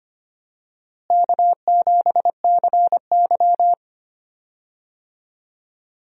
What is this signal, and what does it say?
Morse code, heard as a keyed tone, about 25 wpm, 700 Hz. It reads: K7CY
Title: USA Call Signs - 4 Characters - Repeated 3x - 25wpm